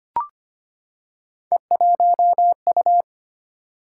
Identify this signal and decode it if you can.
Morse code, heard as a keyed tone, about 25 wpm, 700 Hz. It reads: E1U
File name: Three Letter-Number Combinations - 25wpm